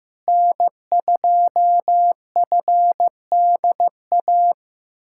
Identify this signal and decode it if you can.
Morse code, heard as a keyed tone, about 15 wpm, 700 Hz. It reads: N2FDA
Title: USA Callsign DE Callsign 15wpm